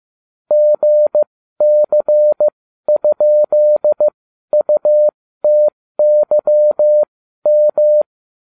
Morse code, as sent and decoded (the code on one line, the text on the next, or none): --. -.-. ..--.. ..- - -.-- --
GC?UTYM